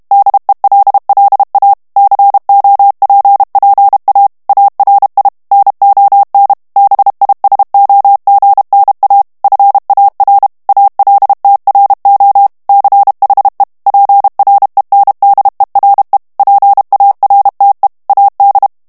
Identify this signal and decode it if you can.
Morse code, heard as a keyed tone, about 32 wpm, 800 Hz. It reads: DELLA COPPA ARI NON BISOGNA FAR ALTRO CHE PRENDERE PARTE AD
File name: ITALIEN texte clair 2 vitesse 30 mots minute